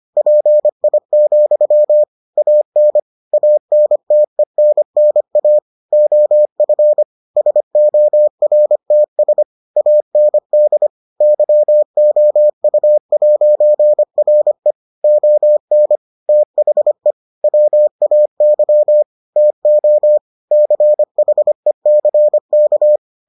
Morse code, heard as a keyed tone, about 25 wpm, 600 Hz. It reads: PI, AN ANTENNA OF SORTS AND YOU'RE ON THE WAY TO CHECK